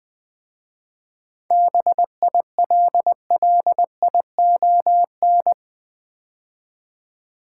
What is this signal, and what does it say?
Morse code, heard as a keyed tone, about 20 wpm, 700 Hz. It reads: BILLION
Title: 7 Letter Words - First 3 Letters Given - 20wpm